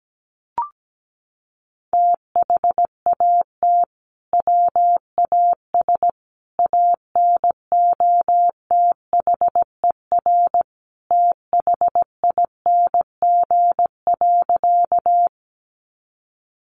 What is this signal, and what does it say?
Morse code, heard as a keyed tone, about 17 wpm, 700 Hz. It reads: THAT WAS ANOTHER THING.
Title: Sentences from Top 600 Words 17wpm